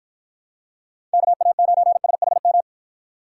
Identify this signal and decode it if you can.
Morse code, heard as a keyed tone, about 48 wpm, 700 Hz. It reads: KA9SHM